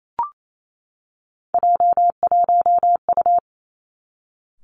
Morse code, heard as a keyed tone, about 28 wpm, 700 Hz. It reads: J1U
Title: Three Challenging Letter-Number Combinations 28wpm